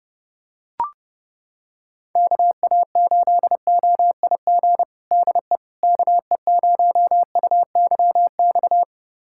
Morse code, heard as a keyed tone, about 30 wpm, 700 Hz. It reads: KA8OIG DE KE0UYX
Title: USA Callsign DE Callsign - 6 Characters - 30wpm